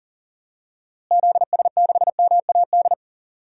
Morse code, heard as a keyed tone, about 38 wpm, 700 Hz. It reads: ZS6MAD
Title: DXCC-100 Call Signs - 6 Characters - Repeated 2x - 40wpm